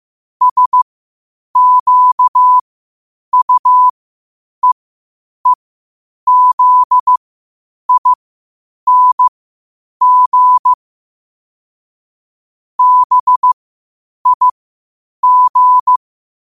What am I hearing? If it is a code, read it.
Morse code, heard as a keyed tone, about 15 wpm, spaced as for 8 wpm, 1000 Hz. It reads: SQUEEZING BIG
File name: qotdfiles / 8wpm